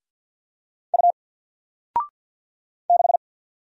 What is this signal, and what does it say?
Morse code, heard as a keyed tone, about 50 wpm, 700 Hz. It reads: U 6
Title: Single Character (Letter:Number:Punctuation:Prosign) 50wpm